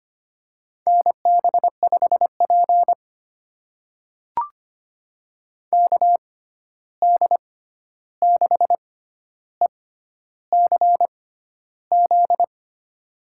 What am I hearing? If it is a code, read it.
Morse code, heard as a keyed tone, about 25 wpm, spaced as for 11 wpm, 700 Hz. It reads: NB5P KD6ECZ
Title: USA Call Signs - 6x Character Spacing - 25wpm